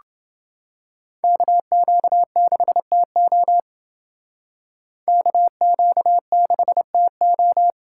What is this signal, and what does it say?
Morse code, heard as a keyed tone, about 28 wpm, 700 Hz. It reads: KQ6TO KQ6TO